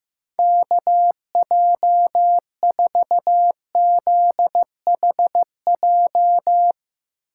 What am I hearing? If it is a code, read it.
Morse code, heard as a keyed tone, about 15 wpm, 700 Hz. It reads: KJ4ZHJ
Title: USA Call Signs - 6 Characters - 8x Character Spacing - 15wpm